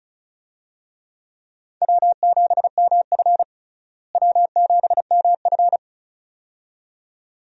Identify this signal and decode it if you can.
Morse code, heard as a keyed tone, about 34 wpm, 700 Hz. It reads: W7MF W7MF
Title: USA Call Signs - 4 Characters - Repeated 2x - 35wpm